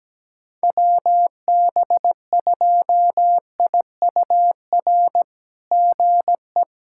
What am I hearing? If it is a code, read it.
Morse code, heard as a keyed tone, about 17 wpm, 700 Hz. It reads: WB2IUR GE
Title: SOTA QSO Practice 17wpm